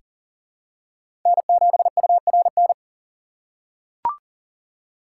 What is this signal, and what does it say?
Morse code, heard as a keyed tone, about 40 wpm, 700 Hz. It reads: N7URN
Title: USA Call Signs - 6x Character Spacing - 40wpm